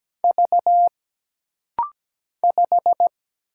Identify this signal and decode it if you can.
Morse code, heard as a keyed tone, about 17 wpm, 700 Hz. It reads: V 5
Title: Single Letter-Number - Rapid-Fire 17wpm